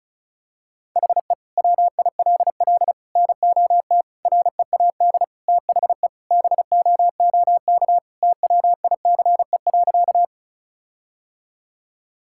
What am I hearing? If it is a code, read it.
Morse code, heard as a keyed tone, about 34 wpm, 700 Hz. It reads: HE WILL NOT READ THE BOOK TWICE.
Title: Sentences from Top 1600 Words 35wpm